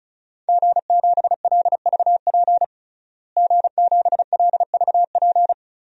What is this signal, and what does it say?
Morse code, heard as a keyed tone, about 34 wpm, 700 Hz. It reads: G7LVP G7LVP
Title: DXCC-100 Call Signs - 5 Characters - Repeated 2x - 35wpm